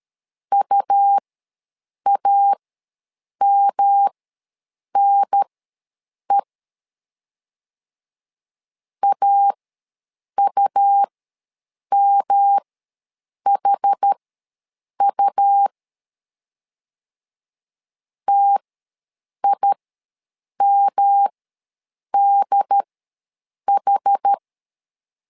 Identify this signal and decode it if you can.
Morse code, heard as a keyed tone, about 13 wpm, spaced as for 7 wpm, 800 Hz. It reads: UAMNE AUMHU TIMDH
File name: LEZIONI / 3 MWRO / 3 M ws 12wpm 1745